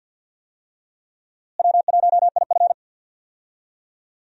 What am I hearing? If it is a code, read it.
Morse code, heard as a keyed tone, about 51 wpm, 700 Hz. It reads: W1IF
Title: USA Call Signs - 4 Characters - Repeated 3x - 50wpm